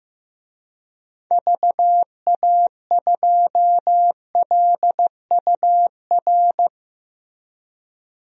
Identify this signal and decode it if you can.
Morse code, heard as a keyed tone, about 15 wpm, 700 Hz. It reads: VA2LUR
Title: DXCC-100 Call Signs - 6 Characters - 15wpm